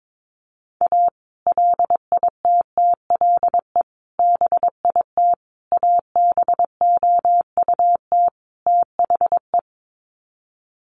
Morse code, heard as a keyed tone, about 22 wpm, 700 Hz. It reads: A LITTLE BIT ABOUT THE
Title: Sets of 5 Words Encore2 22wpm